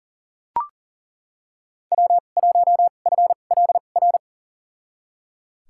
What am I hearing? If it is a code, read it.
Morse code, heard as a keyed tone, about 40 wpm, spaced as for 29 wpm, 700 Hz. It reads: W1FLR
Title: USA Call Signs - 2x Character Spacing - 40wpm